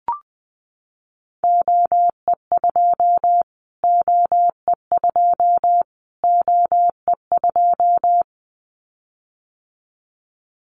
Morse code, heard as a keyed tone, about 20 wpm, 700 Hz. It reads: OE2 OE2 OE2
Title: Callsign Prefixes for Top 25 DXCC - Repeated 3x - 20wpm